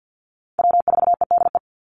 Morse code, heard as a keyed tone, about 50 wpm, 700 Hz. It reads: W4EDE
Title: Call Signs - 50wpm